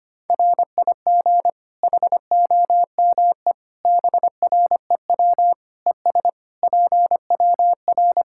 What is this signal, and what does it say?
Morse code, heard as a keyed tone, about 25 wpm, 700 Hz. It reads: RIG HOME BREW ES PWR
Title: Practice QSO - Exchange 2 - Info - 25wpm